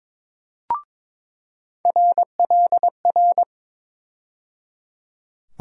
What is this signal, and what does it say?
Morse code, heard as a keyed tone, about 22 wpm, 700 Hz. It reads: RLR
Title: USA Call Sign Postfixes - 3 Letters 22wpm